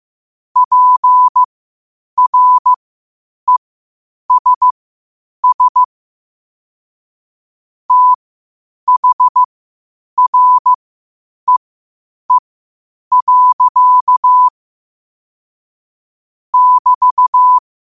Morse code, heard as a keyed tone, about 15 wpm, spaced as for 8 wpm, 1000 Hz. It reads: PRESS THREE. <BT>
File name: qotdfiles / 8wpm